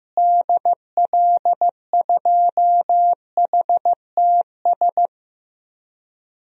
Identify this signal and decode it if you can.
Morse code, heard as a keyed tone, about 15 wpm, 700 Hz. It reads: DL2HTS
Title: DXCC-100 Call Signs - 6 Characters - Repeated 3x - 15wpm